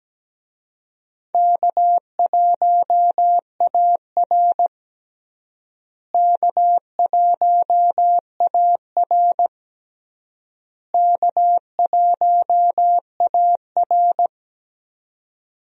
Morse code, heard as a keyed tone, about 17 wpm, 700 Hz. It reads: K1AR K1AR K1AR
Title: USA Call Signs - 4 Characters - Repeated 3x - 17wpm